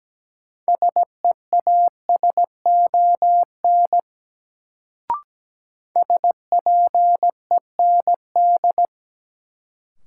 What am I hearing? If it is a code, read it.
Morse code, heard as a keyed tone, about 17 wpm, 700 Hz. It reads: SEASON SPEND
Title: Sets of 5 Words - New Word Drill - 17wpm